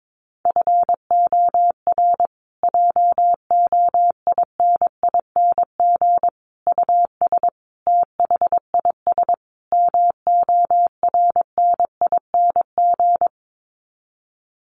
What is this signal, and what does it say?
Morse code, heard as a keyed tone, about 22 wpm, 700 Hz. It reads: FOR JOINING US THIS MORNING
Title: Sets of 5 Words Encore 22wpm